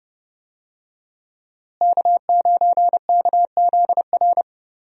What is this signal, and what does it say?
Morse code, heard as a keyed tone, about 32 wpm, 700 Hz. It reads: K9KZR
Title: USA Call Signs - 5 Characters - Repeated 2x - 30wpm